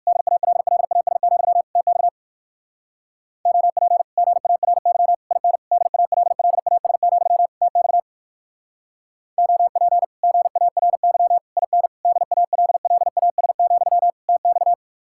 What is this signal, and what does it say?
Morse code, heard as a keyed tone, about 62 wpm, 700 Hz. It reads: DALLAS, TX OP GARY IN DALLAS, TX OP GARY IN DALLAS, TX